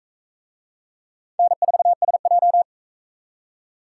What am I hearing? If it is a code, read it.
Morse code, heard as a keyed tone, about 40 wpm, 700 Hz. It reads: N4SJ